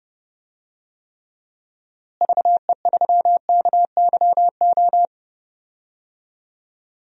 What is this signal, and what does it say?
Morse code, heard as a keyed tone, about 30 wpm, 700 Hz. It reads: VE3KYO
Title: DXCC-100 Call Signs - 6 Characters - Repeated 2x - 30wpm